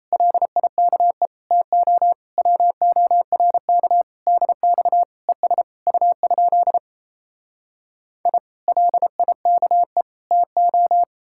Morse code, heard as a keyed tone, about 38 wpm, 700 Hz. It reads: LIKE TO WORK DX ES U? I LIKE TO